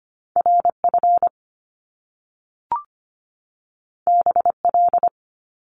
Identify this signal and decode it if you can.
Morse code, heard as a keyed tone, about 25 wpm, 700 Hz. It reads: R F B L
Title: Top Two Letter Combinations in Words 25wpm